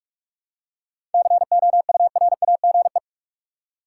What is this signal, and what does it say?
Morse code, heard as a keyed tone, about 44 wpm, 700 Hz. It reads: COURAGE
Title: Seven Letter Words - 45wpm